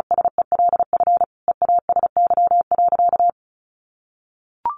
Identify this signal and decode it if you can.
Morse code, heard as a keyed tone, about 34 wpm, 700 Hz. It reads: SELF EASY.
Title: Sentences from Top 700 Words 35wpm